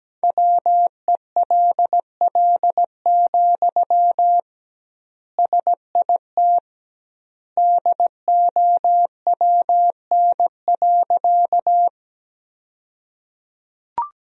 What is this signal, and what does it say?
Morse code, heard as a keyed tone, about 17 wpm, 700 Hz. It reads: WELL, SIT DOWN.
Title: 2x Word-Spacing - Sentences from Top 500 Words - 17wpm